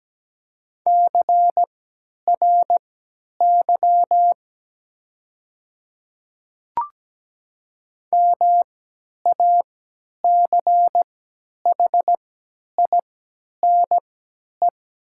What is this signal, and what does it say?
Morse code, heard as a keyed tone, about 17 wpm, spaced as for 10 wpm, 700 Hz. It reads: CRY MACHINE
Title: 3x Character Spacing - Top 500 Words - 17wpm